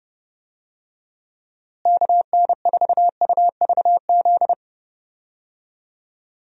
Morse code, heard as a keyed tone, about 30 wpm, 700 Hz. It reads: KN4UVZ
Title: USA Call Signs - 6 Characters - Repeated 2x - 30wpm